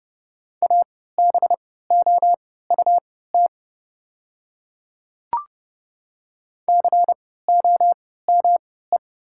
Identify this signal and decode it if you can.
Morse code, heard as a keyed tone, about 31 wpm, spaced as for 17 wpm, 700 Hz. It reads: ABOUT COME
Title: 3x Character Spacing - Top 300 Words - 30wpm